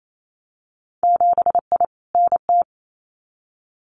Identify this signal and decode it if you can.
Morse code, heard as a keyed tone, about 28 wpm, 700 Hz. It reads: 7I NT
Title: Winter Field Day Exchanges Encore 28wpm